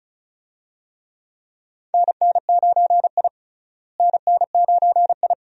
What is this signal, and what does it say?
Morse code, heard as a keyed tone, about 36 wpm, 700 Hz. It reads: NN9I NN9I
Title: USA Call Signs - 4 Characters - Repeated 2x - 35wpm